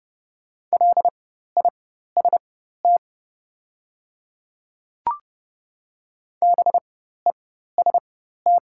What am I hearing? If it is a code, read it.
Morse code, heard as a keyed tone, about 30 wpm, spaced as for 14 wpm, 690 Hz. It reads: LIST BEST
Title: 4x Character Spacing - Top 500 Words - 30wpm